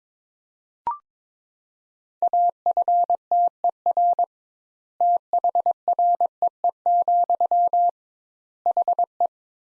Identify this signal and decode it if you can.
Morse code, heard as a keyed tone, about 22 wpm, 700 Hz. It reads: AFTER THREE, HE
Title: Sentences Top 200 Words - 22wpm